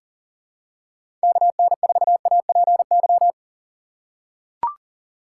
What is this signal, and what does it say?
Morse code, heard as a keyed tone, about 38 wpm, 700 Hz. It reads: KN4APY
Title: USA Call Signs - 2x Character Spacing - 40wpm